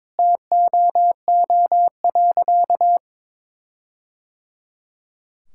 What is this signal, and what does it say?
Morse code, heard as a keyed tone, about 22 wpm, 700 Hz. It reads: TOO.
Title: Sentences from Top 1600 Words 22wpm